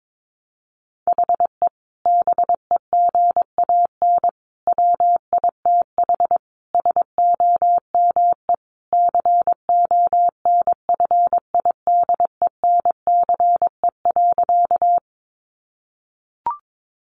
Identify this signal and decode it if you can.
Morse code, heard as a keyed tone, about 22 wpm, 700 Hz. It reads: HE BEGAN WITH SOME CONFIDENCE.
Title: Sentences from Top 1500 Words 22wpm